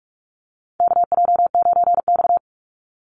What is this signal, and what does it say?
Morse code, heard as a keyed tone, about 44 wpm, 700 Hz. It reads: KJ9X
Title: NA SOTA Chaser Callsigns 45wpm